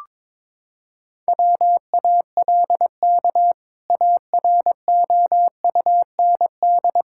Morse code, heard as a keyed tone, about 22 wpm, 700 Hz. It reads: WALK AROUND
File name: practice / 22wpm / Sentences with 2-Word Phrases - 22wpm